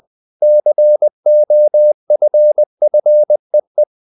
Morse code, heard as a keyed tone, about 20 wpm, 600 Hz. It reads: COFFEE
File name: the old man and the sea / the old man and the sea part 1 20wpm